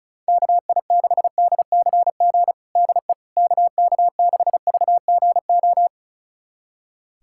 Morse code, heard as a keyed tone, about 34 wpm, 700 Hz. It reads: KI6DCG DE KK6VGO
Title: USA Callsign DE Callsign - 6 Characters - 35wpm